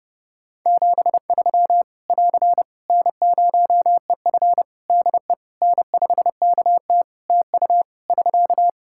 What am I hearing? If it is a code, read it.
Morse code, heard as a keyed tone, about 30 wpm, 700 Hz. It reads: 73 <AR> N0EF DE N5KT TU <SK>
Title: Practice QSO - Exchange 4 - Ending - 30wpm